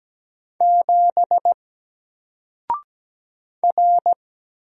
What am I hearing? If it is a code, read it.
Morse code, heard as a keyed tone, about 17 wpm, 700 Hz. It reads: 7 R
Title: Single Character (Letter:Number:Punctuation:Prosign) Alt 17wpm